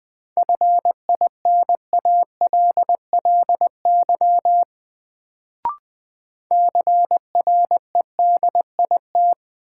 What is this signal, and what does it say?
Morse code, heard as a keyed tone, about 20 wpm, 700 Hz. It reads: FINALLY CREDIT
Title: Sentences from Top 1000 Words - New Word Drill - 20wpm